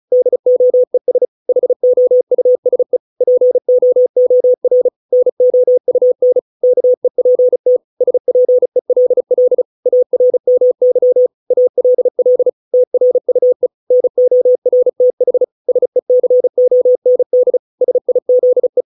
Morse code, heard as a keyed tone, about 35 wpm, 500 Hz. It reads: DOES HOUSE POOR NOUN KEPT SPELL ARMY ALL TRUE NORTH SECOND SIZE
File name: 632 words / 632 Words 35WPM